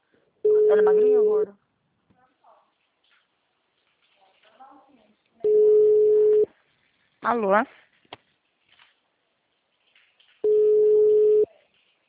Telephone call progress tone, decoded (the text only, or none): ringback tone